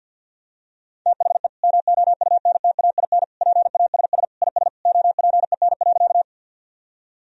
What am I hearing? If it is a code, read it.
Morse code, heard as a keyed tone, about 46 wpm, 700 Hz. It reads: THE MOUNTAIN PASS IS OPEN.
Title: Sentences from Top 1000 Words - Review Part 1 - 50wpm